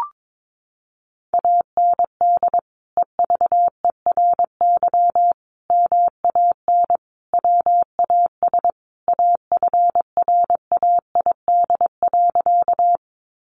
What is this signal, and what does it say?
Morse code, heard as a keyed tone, about 22 wpm, 700 Hz. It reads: AND EVERY MAN WAS AFRAID.
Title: Sentences from Top 1500 Words 22wpm